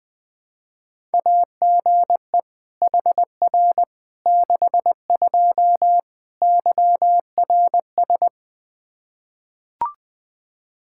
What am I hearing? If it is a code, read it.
Morse code, heard as a keyed tone, about 20 wpm, 700 Hz. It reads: AGE HR 62 YRS
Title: Practice QSO - Exchange 3 - More Info - 20wpm